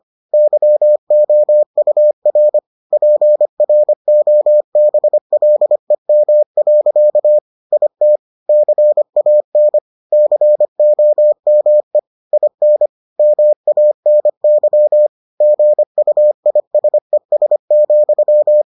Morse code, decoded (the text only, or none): YOUR PROBLEM. IT CAN COME IN MANY GUISES,